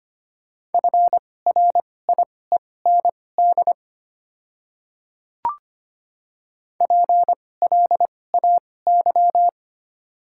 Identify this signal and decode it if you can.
Morse code, heard as a keyed tone, about 25 wpm, spaced as for 18 wpm, 700 Hz. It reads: FRIEND PLAY
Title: 2x Character Spacing - Top 300 Words - 25wpm